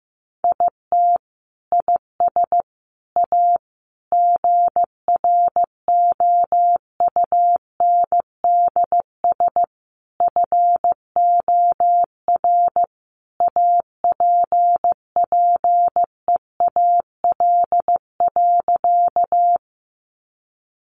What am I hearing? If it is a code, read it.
Morse code, heard as a keyed tone, about 15 wpm, 700 Hz. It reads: IT IS A GROUNDS FOR APPEAL.
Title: Sentences from Top 1300 Words 15wpm